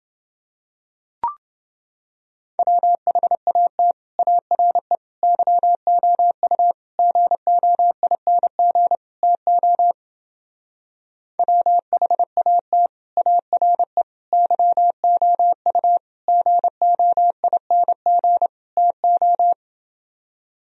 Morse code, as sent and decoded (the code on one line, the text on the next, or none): .-- .... .- - / .- .-. / -.-- --- ..- / --. --- .. -. --. / - --- / .-- .... .- - / .- .-. / -.-- --- ..- / --. --- .. -. --. / - ---
WHAT ARE YOU GOING TO WHAT ARE YOU GOING TO